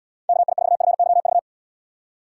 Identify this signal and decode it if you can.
Morse code, heard as a keyed tone, about 77 wpm, 700 Hz. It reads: KI4FYV